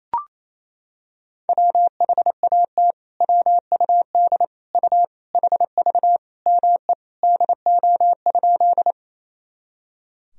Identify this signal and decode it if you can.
Morse code, heard as a keyed tone, about 28 wpm, 700 Hz. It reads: WHAT WUD U HV ME DO?
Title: Sentences from Top 200 Words and Abbreviations 28wpm